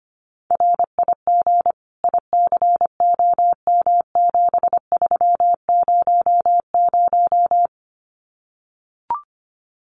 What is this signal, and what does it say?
Morse code, heard as a keyed tone, about 25 wpm, 700 Hz. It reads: RIG ICOM7300
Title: Common QSO Phrases 25wpm